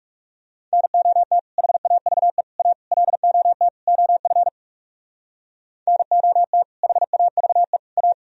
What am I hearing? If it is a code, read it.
Morse code, heard as a keyed tone, about 37 wpm, 700 Hz. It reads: NOT HAVE A LOT OF NOT HAVE A
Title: Speed-Racing - Sets of 5 Words 30wpm